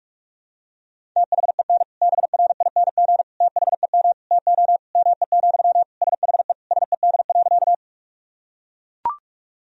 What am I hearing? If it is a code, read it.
Morse code, heard as a keyed tone, about 42 wpm, 690 Hz. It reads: THEN BRING THEM TO ME, SHE SED.
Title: Sentences from Top 500 Words and Abbreviations 45wpm